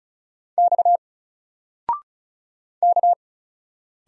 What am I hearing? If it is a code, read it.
Morse code, heard as a keyed tone, about 35 wpm, 700 Hz. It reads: X K
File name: Single-Letter-Number-35wpm